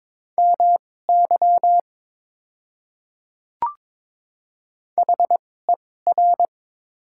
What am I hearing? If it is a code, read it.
Morse code, heard as a keyed tone, about 22 wpm, spaced as for 16 wpm, 700 Hz. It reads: MY HER
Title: 2x Character Spacing - Top 100 Words - 22wpm